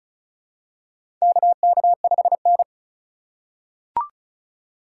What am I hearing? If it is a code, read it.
Morse code, heard as a keyed tone, about 36 wpm, 700 Hz. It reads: KK5N
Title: USA Call Signs - 4x Character Spacing - 35wpm